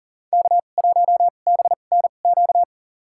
Morse code, heard as a keyed tone, about 39 wpm, spaced as for 28 wpm, 700 Hz. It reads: K1BNQ
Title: USA Call Signs - 5 Characters - 2x Character Spacing - 40wpm